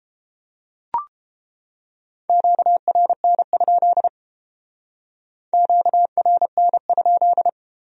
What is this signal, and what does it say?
Morse code, heard as a keyed tone, about 30 wpm, 700 Hz. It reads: QRN? QRN?